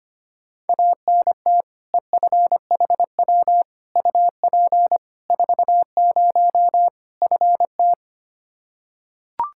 Morse code, heard as a keyed tone, about 25 wpm, 700 Hz. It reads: ANT EFHW UP 40 FT
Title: Practice QSO - Exchange 2 - Info - 25wpm